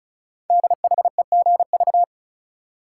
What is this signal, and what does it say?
Morse code, heard as a keyed tone, about 36 wpm, 700 Hz. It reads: DHEGV
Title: Code Groups - 5 Letters - 35wpm